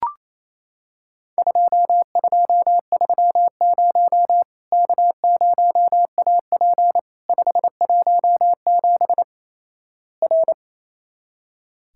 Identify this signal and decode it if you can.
Morse code, heard as a keyed tone, about 28 wpm, 700 Hz. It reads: 2230 K0AP 517 R
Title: Worked All Europe DX Contest - QTC Exchange - 28wpm